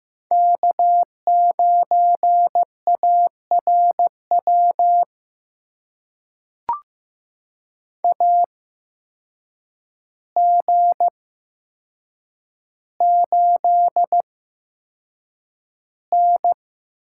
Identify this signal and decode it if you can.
Morse code, heard as a keyed tone, about 15 wpm, spaced as for 7 wpm, 700 Hz. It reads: K9ARW AG8N
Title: USA Call Signs - 8x Character Spacing - 15wpm